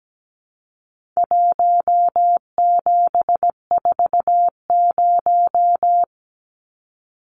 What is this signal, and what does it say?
Morse code, heard as a keyed tone, about 17 wpm, 700 Hz. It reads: 1740
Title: Numbers - 4-Digit 17wpm